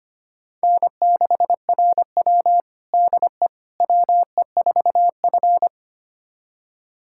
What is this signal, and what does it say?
Morse code, heard as a keyed tone, about 25 wpm, 700 Hz. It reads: N6RW DE WE4F
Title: USA Callsign DE Callsign - 4 Characters - 25wpm